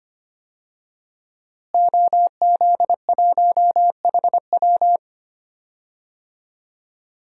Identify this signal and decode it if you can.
Morse code, heard as a keyed tone, about 25 wpm, 700 Hz. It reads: OZ1HW
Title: DXCC-100 Call Signs - 5 Characters - Repeated 3x - 25wpm